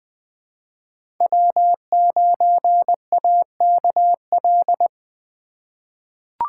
Morse code, heard as a keyed tone, about 20 wpm, 700 Hz. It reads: W9AKL
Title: USA Call Signs - 5 Characters - 2x Character Spacing - 20wpm